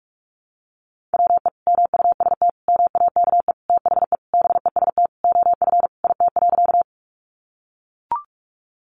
Morse code, heard as a keyed tone, about 44 wpm, 700 Hz. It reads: WE MUST MAKE THE BEST OF IT.
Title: Sentences from Top 400 Words 45wpm